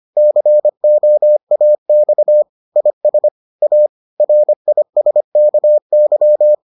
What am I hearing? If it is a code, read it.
Morse code, heard as a keyed tone, about 25 wpm, 600 Hz. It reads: COAX IS A RISKY